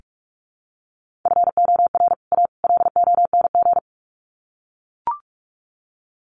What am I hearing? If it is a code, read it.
Morse code, heard as a keyed tone, about 44 wpm, 700 Hz. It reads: FOR A LONG
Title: Sets of 3 Words 45wpm